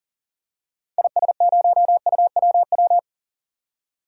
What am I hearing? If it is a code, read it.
Morse code, heard as a keyed tone, about 40 wpm, 700 Hz. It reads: IS0UWW
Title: DXCC-100 Call Signs - 6 Characters - 40wpm